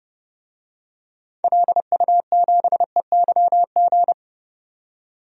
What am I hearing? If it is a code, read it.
Morse code, heard as a keyed tone, about 30 wpm, 700 Hz. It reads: LU7EYG
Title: DXCC-100 Call Signs - 6 Characters - 30wpm